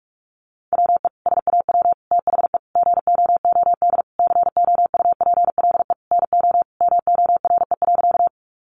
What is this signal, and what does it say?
Morse code, heard as a keyed tone, about 44 wpm, 700 Hz. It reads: WE SAW THE GOOD COUPLE NO MORE.
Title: Sentences from Top 1100 Words 45wpm